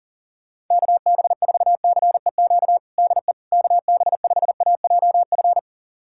Morse code, heard as a keyed tone, about 40 wpm, 700 Hz. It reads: KB4CEQ DE KB5AJF
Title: USA Callsign DE Callsign 40wpm